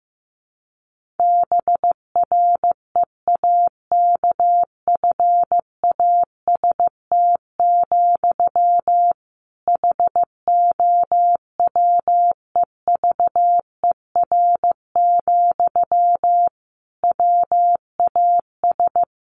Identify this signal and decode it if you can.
Morse code, heard as a keyed tone, about 15 wpm, 700 Hz. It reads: BREAKFAST, HOWEVER, WAS
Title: Sentences from Top 1500 Words 15wpm